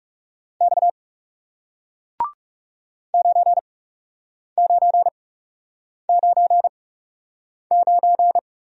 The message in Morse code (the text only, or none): X 9999